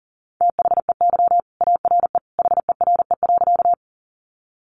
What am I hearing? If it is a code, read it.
Morse code, heard as a keyed tone, about 40 wpm, 700 Hz. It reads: THEY ARE HERE.
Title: Sentences from Top 200 Words 40wpm